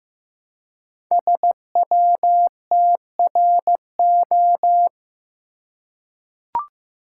Morse code, heard as a keyed tone, about 15 wpm, 700 Hz. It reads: SWTRO
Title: Code Groups - 5 Letters - 15wpm